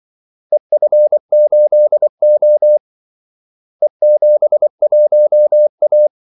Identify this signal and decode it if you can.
Morse code, heard as a keyed tone, about 24 wpm, 600 Hz. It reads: EF8O E71A
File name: Callsigns-24wpm